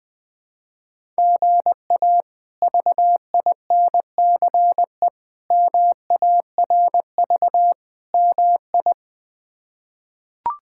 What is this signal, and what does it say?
Morse code, heard as a keyed tone, about 20 wpm, 700 Hz. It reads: GA VINCE MARV MI
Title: K1USN SST QSO Practice - 20wpm